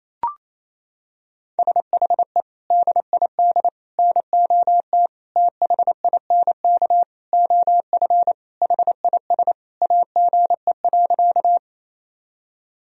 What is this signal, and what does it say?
Morse code, heard as a keyed tone, about 28 wpm, 700 Hz. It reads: SHE DID NOT THINK OF HIS AGE.
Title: Sentences from Top 1000 Words - Review Part 3 - 28wpm